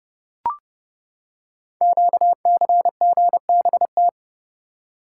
Morse code, heard as a keyed tone, about 28 wpm, 700 Hz. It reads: QCGBT
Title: Code Groups - 5 Letter:Numbers - 30wpm